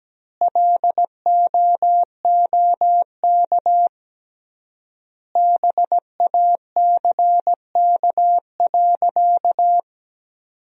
Morse code, heard as a keyed tone, about 17 wpm, 700 Hz. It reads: LOOK BACK.
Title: 3x Word-Spacing - Sentences from Top 300 Words - 17wpm